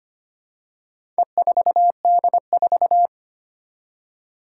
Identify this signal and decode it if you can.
Morse code, heard as a keyed tone, about 25 wpm, 700 Hz. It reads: E4D4
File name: practice / 25wpm / Four Letter-Number Combinations - 25wpm